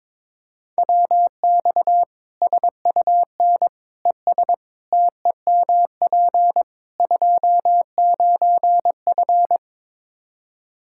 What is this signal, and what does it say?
Morse code, heard as a keyed tone, about 22 wpm, 700 Hz. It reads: WX SUN ES TEMP 29F
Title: Practice QSO - Exchange 2 - Info - 22wpm